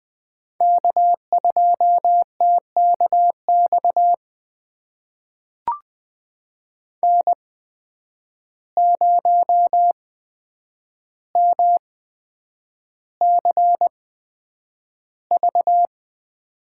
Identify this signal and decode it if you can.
Morse code, heard as a keyed tone, about 20 wpm, spaced as for 9 wpm, 700 Hz. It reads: K2TKX N0MCV
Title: USA Call Signs - 5 Characters - 8x Character Spacing - 20wpm